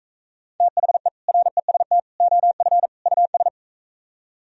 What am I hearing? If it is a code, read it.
Morse code, heard as a keyed tone, about 42 wpm, 700 Hz. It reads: THE REST OF US